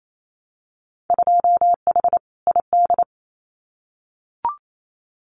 Morse code, heard as a keyed tone, about 28 wpm, 700 Hz. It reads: 2H ID
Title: Winter Field Day Exchanges 28wpm